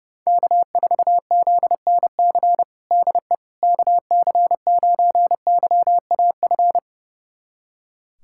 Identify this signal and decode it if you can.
Morse code, heard as a keyed tone, about 30 wpm, 700 Hz. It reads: K4ZNC DE KC9YAF
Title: USA Callsign DE Callsign 30wpm